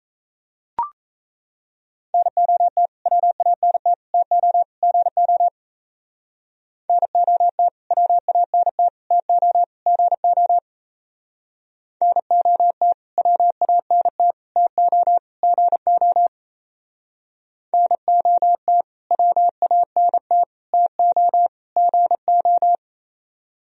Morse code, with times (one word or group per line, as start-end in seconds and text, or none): 2.1-2.9 NOT
3.1-3.9 WANT
4.1-4.6 TO
4.8-5.5 GO
6.9-7.7 NOT
7.9-8.9 WANT
9.1-9.6 TO
9.9-10.6 GO
12.0-12.9 NOT
13.2-14.3 WANT
14.6-15.2 TO
15.4-16.3 GO
17.7-18.8 NOT
19.1-20.4 WANT
20.7-21.5 TO
21.8-22.8 GO